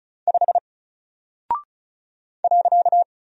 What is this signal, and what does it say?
Morse code, heard as a keyed tone, about 35 wpm, 700 Hz. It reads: 5 .
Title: Single Character (Letter:Number:Punctuation:Prosign) Alt 35wpm